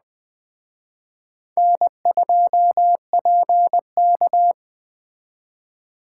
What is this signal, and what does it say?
Morse code, heard as a keyed tone, about 23 wpm, 700 Hz. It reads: N2PK
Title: Speed Racing - USA 1x2 Call Signs 20wpm